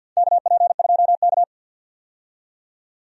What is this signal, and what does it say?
Morse code, heard as a keyed tone, about 52 wpm, 700 Hz. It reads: KP2K
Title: DXCC-100 Call Signs - 4 Characters - 50wpm